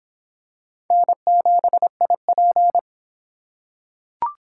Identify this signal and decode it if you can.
Morse code, heard as a keyed tone, about 25 wpm, 700 Hz. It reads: N7IP